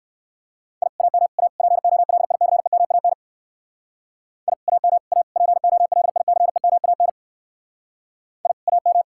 Morse code, heard as a keyed tone, about 59 wpm, 700 Hz. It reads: I AM A POLICEMAN I AM A POLICEMAN I AM